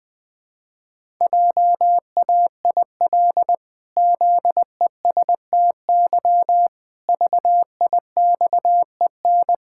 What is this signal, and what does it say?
Morse code, heard as a keyed tone, about 20 wpm, 700 Hz. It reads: JAIL ZESTY VIXEN
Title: Pangram Challenge v2 - 20wpm